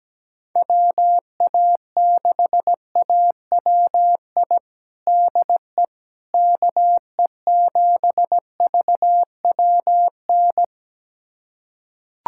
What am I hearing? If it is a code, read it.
Morse code, heard as a keyed tone, about 17 wpm, 700 Hz. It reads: WA6AWI DE KE7VWN
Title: USA Callsign DE Callsign - 6 Characters - 17wpm